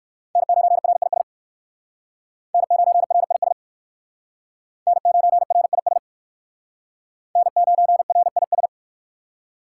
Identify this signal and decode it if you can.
Morse code, heard as a keyed tone, about 59 wpm, 700 Hz. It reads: N9RIS N9RIS N9RIS N9RIS